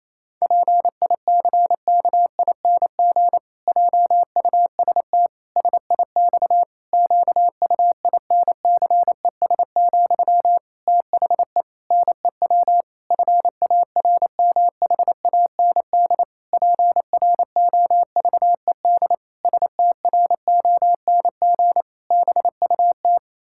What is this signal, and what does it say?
Morse code, heard as a keyed tone, about 28 wpm, 700 Hz. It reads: PICKING JUST SIX QUINCES, THE NEW FARMHAND PROVED STRONG BUT
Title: Pangram Challenge v2 - 28wpm